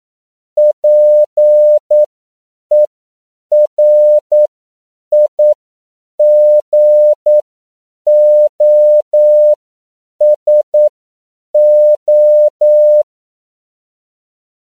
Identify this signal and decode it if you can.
Morse code, heard as a keyed tone, about 9 wpm, spaced as for 7 wpm, 600 Hz. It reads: PERIGOSO